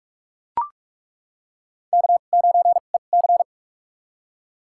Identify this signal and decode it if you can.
Morse code, heard as a keyed tone, about 42 wpm, spaced as for 31 wpm, 700 Hz. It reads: K9EC
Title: USA Call Signs - 2x Character Spacing - 45wpm